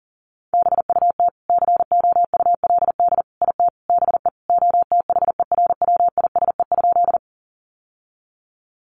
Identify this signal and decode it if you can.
Morse code, heard as a keyed tone, about 40 wpm, 700 Hz. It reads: BUT COULD IT BE OTHERWISE?
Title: Sentences from Top 1200 Words 40wpm